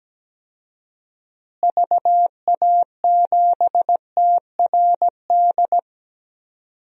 Morse code, heard as a keyed tone, about 17 wpm, 700 Hz. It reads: VA7TRD
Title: DXCC-100 Call Signs - 6 Characters - Repeated 2x - 17wpm